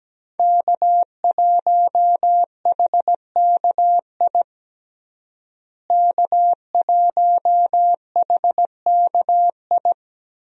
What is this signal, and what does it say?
Morse code, heard as a keyed tone, about 17 wpm, 700 Hz. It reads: K1HKI K1HKI
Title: USA Call Signs - 5 Characters - Repeated 2x - 17wpm